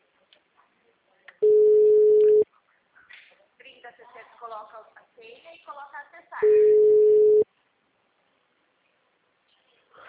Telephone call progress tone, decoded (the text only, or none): ringback tone